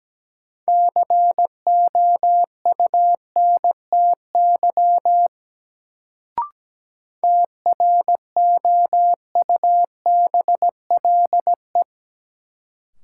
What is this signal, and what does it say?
Morse code, heard as a keyed tone, about 17 wpm, 700 Hz. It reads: COUNTY TROUBLE
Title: Sentences from Top 1200 Words - New Word Drill Alt - 17wpm